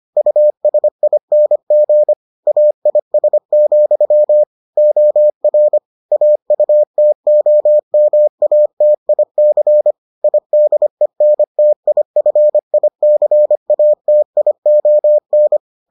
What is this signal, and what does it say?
Morse code, heard as a keyed tone, about 25 wpm, 600 Hz. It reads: USING AIS, OR AUTOMATIC IDENTIFICATION